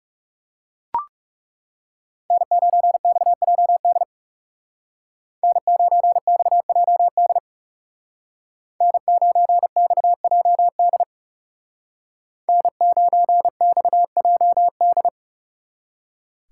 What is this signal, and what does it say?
Morse code, heard as a keyed tone, about 40 wpm, 700 Hz. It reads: N9XJD N9XJD N9XJD N9XJD